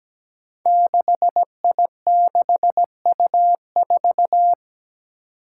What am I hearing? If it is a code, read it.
Morse code, heard as a keyed tone, about 17 wpm, 700 Hz. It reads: 6I6U4
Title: Five Challenging Letter-Number Combinations 17wpm